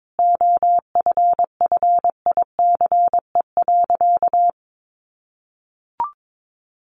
Morse code, heard as a keyed tone, about 22 wpm, 700 Hz. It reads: OFFICE.
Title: Sentences from Top 1000 Words 22wpm